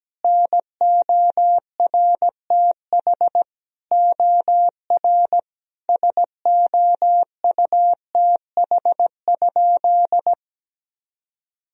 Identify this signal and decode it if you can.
Morse code, heard as a keyed tone, about 17 wpm, 700 Hz. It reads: NORTH OR SOUTH?
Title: Sentences from Top 1000 Words - Review Part 2 - 17wpm